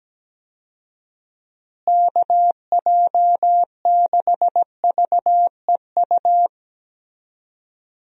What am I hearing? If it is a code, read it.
Morse code, heard as a keyed tone, about 17 wpm, 700 Hz. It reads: KJ6VEU
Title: USA Call Signs - 6 Characters - Repeated 3x - 17wpm